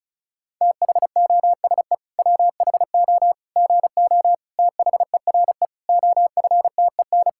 Morse code, heard as a keyed tone, about 36 wpm, 700 Hz. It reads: THOSE WHO GO THERE OFTEN
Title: Sentences from Top 1000 Words - Review Part 5 - 35wpm